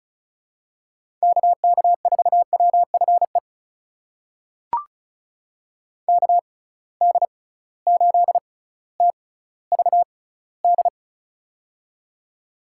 Morse code, heard as a keyed tone, about 34 wpm, spaced as for 18 wpm, 700 Hz. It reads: KK4WFE KD8TVD
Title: USA Call Signs - 6 Characters - 6x Character Spacing - 35wpm